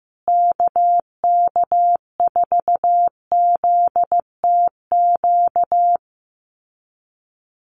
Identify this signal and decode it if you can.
Morse code, heard as a keyed tone, about 14 wpm, 700 Hz. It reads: KK4ZTQ
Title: USA Call Signs Repeated 3x 15wpm